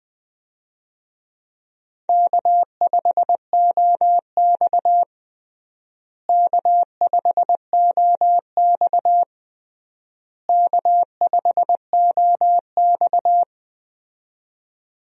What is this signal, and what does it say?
Morse code, heard as a keyed tone, about 20 wpm, 700 Hz. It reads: K5OX K5OX K5OX
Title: USA Call Signs - 4 Characters - Repeated 3x - 20wpm